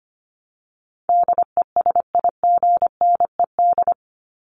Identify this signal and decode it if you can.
Morse code, heard as a keyed tone, about 25 wpm, 700 Hz. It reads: DESIGNED
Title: ED Suffix - Top 100 Words with Base Words 25wpm